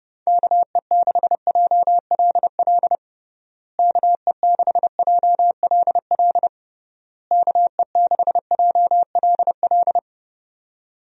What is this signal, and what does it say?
Morse code, heard as a keyed tone, about 30 wpm, 700 Hz. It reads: KE6JLL KE6JLL KE6JLL
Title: USA Call Signs - 6 Characters - Repeated 3x - 30wpm